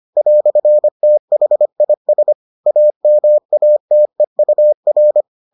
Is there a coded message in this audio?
Yes: Morse code from 0.2 to 5.2 s, heard as a keyed tone, about 25 wpm, 600 Hz. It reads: "THIS AMATEUR